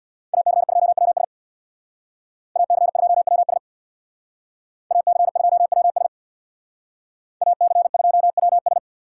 Morse code, heard as a keyed tone, about 63 wpm, 700 Hz. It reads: AC2WS AC2WS AC2WS AC2WS